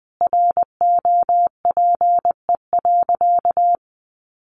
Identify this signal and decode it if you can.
Morse code, heard as a keyed tone, about 20 wpm, 700 Hz. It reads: ROPE.
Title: Easy Phrases 20wpm